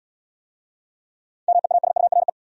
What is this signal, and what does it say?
Morse code, heard as a keyed tone, about 80 wpm, 700 Hz. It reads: MEASURE